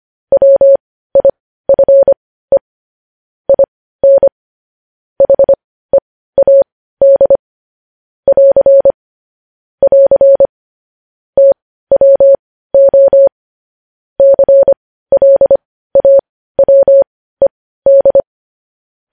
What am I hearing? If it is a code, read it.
Morse code, heard as a keyed tone, about 25 wpm, spaced as for 15 wpm, 550 Hz. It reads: WIFE IN HEAD <AR> <AR> TWO CLAWED
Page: news-headlines-in-morse-code-at-15-wpm